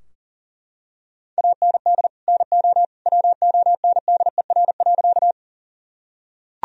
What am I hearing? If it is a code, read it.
Morse code, heard as a keyed tone, about 40 wpm, 700 Hz. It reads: AND NO WONDER.
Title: Sentences from Top 1000 Words - Review Part 3 - 40wpm